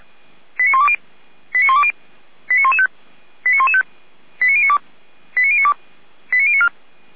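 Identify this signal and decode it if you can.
Five-tone selective call (ZVEI), heard as quick runs of five tones, tones of about 70 ms. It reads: 89120 89120 89115 89115 89092 89092 89094
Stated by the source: ZVEI five-tone call received on an SDR.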